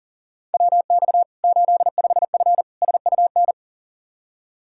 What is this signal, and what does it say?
Morse code, heard as a keyed tone, about 37 wpm, 700 Hz. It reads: WX 85F SUN
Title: Practice QSO - Exchange 2 - Info - 40wpm